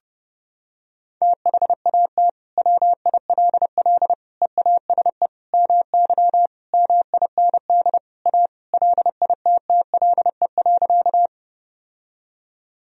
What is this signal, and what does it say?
Morse code, heard as a keyed tone, about 30 wpm, 700 Hz. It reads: THAT WILL EASE MY MIND A LITTLE.
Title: Sentences from Top 1000 Words - Review Part 1 - 30wpm